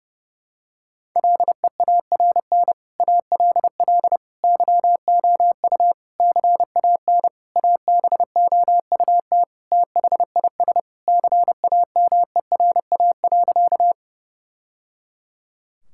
Morse code, heard as a keyed tone, about 30 wpm, 700 Hz. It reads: LEARN ALL YOU CAN ABOUT THIS CAMERA.
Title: Sentences from Top 1000 Words - Review Part 5 - 30wpm